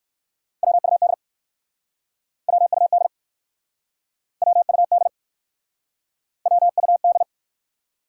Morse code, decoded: WUD WUD WUD WUD

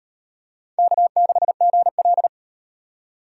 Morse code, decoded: K6GL

